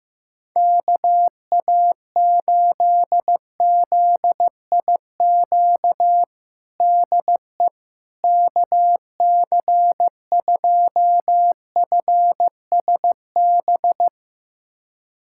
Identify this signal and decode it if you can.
Morse code, heard as a keyed tone, about 15 wpm, 700 Hz. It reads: KA8ZIQ DE KC2FSB
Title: USA Callsign DE Callsign 15wpm